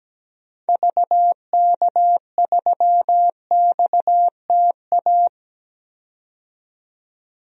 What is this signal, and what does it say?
Morse code, heard as a keyed tone, about 17 wpm, 700 Hz. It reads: VK3XTA
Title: DXCC-100 Call Signs - 6 Characters - 17wpm